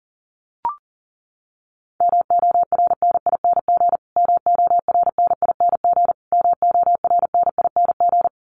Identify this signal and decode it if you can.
Morse code, heard as a keyed tone, about 40 wpm, 700 Hz. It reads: MORNING MORNING MORNING
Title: Top 500 Words Repeated 40wpm